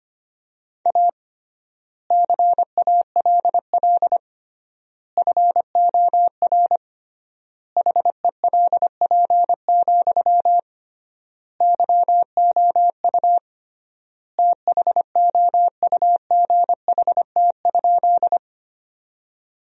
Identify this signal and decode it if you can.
Morse code, heard as a keyed tone, about 25 wpm, 700 Hz. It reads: A CALL FOR HELP, YOU THOUGHT?
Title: 3x Word-Spacing - Sentences from Top 300 Words - 25wpm